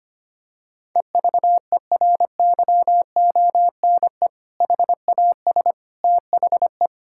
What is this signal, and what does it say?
Morse code, heard as a keyed tone, about 25 wpm, 700 Hz. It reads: EVERYONE HAS THE
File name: practice / 25wpm / Sentences with 3-Word Phrases - 25wpm